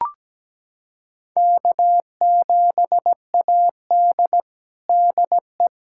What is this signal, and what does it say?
Morse code, heard as a keyed tone, about 17 wpm, 700 Hz. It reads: K7AD DE
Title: USA Callsign DE Callsign - 4 Characters - 17wpm